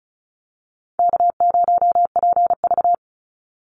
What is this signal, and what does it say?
Morse code, heard as a keyed tone, about 34 wpm, 700 Hz. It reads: K0PV
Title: Easy Call Signs Repeated 2x - 35wpm